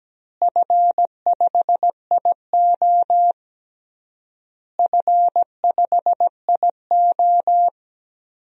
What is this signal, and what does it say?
Morse code, heard as a keyed tone, about 17 wpm, 700 Hz. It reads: F5IO F5IO
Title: DXCC-100 Call Signs - 4 Characters - Repeated 3x - 17wpm